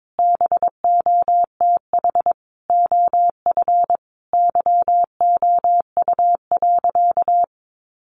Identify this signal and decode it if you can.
Morse code, heard as a keyed tone, about 22 wpm, 700 Hz. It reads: BOTH OF YOU.
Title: Sentences from Top 1100 Words 22wpm